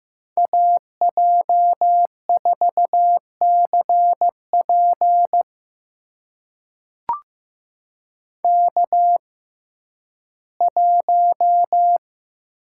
Morse code, heard as a keyed tone, about 15 wpm, spaced as for 11 wpm, 700 Hz. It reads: AJ4CP K1
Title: USA Call Signs - 5 Characters - 6x Character Spacing - 15wpm